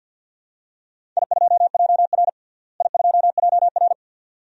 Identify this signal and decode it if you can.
Morse code, heard as a keyed tone, about 53 wpm, 700 Hz. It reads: I2JR I2JR
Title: DXCC-100 Call Signs - 4 Characters - Repeated 2x - 50wpm